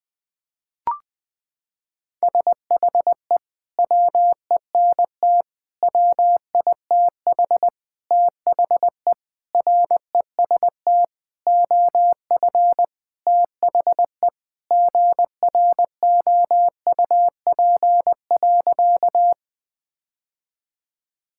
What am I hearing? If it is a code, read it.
Morse code, heard as a keyed tone, about 20 wpm, 700 Hz. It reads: SHE WENT WITH THE REST OF THE GROUP.
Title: Sentences with 5-Word Phrases - 20wpm